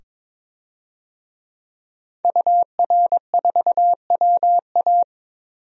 Morse code, heard as a keyed tone, about 21 wpm, 700 Hz. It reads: UR4WA
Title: DXCC-100 Call Signs - 5 Characters - 22wpm